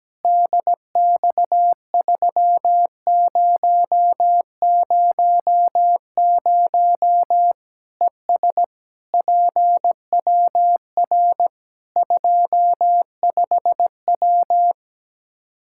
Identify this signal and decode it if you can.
Morse code, heard as a keyed tone, about 17 wpm, 700 Hz. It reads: DX3000 ES PWR 25W
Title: Practice QSO - Exchange 2 - Info - 17wpm